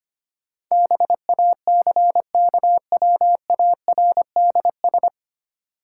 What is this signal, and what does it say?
Morse code, heard as a keyed tone, about 25 wpm, 700 Hz. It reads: BACKWARDS
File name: practice / 25wpm / 9 Letter Words - First 3 Letters Given - 25wpm